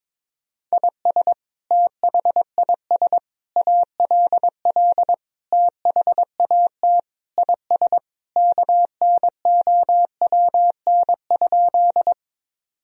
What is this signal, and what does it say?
Morse code, heard as a keyed tone, about 22 wpm, 700 Hz. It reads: IS THIS ALL THAT IS KNOWN?
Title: Sentences from Top 1000 Words - Review Part 3 - 22wpm